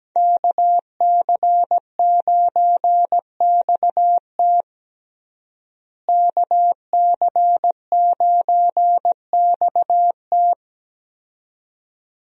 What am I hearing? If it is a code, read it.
Morse code, heard as a keyed tone, about 17 wpm, 700 Hz. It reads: KC9XT KC9XT
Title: USA Call Signs - 5 Characters - Repeated 2x - 17wpm